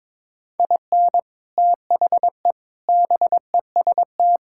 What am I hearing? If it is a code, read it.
Morse code, heard as a keyed tone, about 22 wpm, 700 Hz. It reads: IN THE BEST
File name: practice / 22wpm / Sets of 6 Words - 22wpm